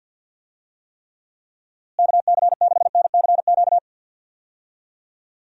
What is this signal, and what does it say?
Morse code, heard as a keyed tone, about 52 wpm, 700 Hz. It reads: KC6NCQ
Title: USA Call Signs - 6 Characters - Repeated 2x - 50wpm